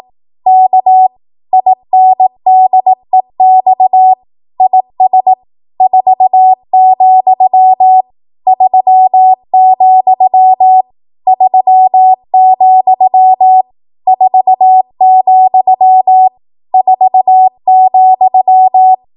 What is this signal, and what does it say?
Morse code, heard as a keyed tone, about 18 wpm, 750 Hz. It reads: K INDEX IS 4, 3, 3, 4, 4,
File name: Morse / 18WPM